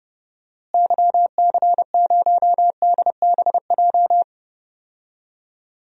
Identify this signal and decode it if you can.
Morse code, heard as a keyed tone, about 30 wpm, 700 Hz. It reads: YC0DBJ
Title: DXCC-100 Call Signs - 6 Characters - Repeated 2x - 30wpm